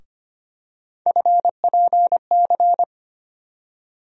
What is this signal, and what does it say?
Morse code, heard as a keyed tone, about 25 wpm, 700 Hz. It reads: FPC
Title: USA Call Sign Postfixes - 3 Letters 25wpm